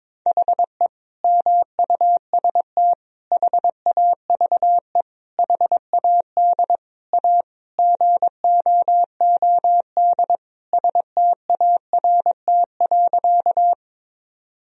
Morse code, heard as a keyed tone, about 22 wpm, 700 Hz. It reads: HE MUST HAVE HAD A GOOD START.
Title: Sentences from Top 1000 Words - Review Part 1 - 22wpm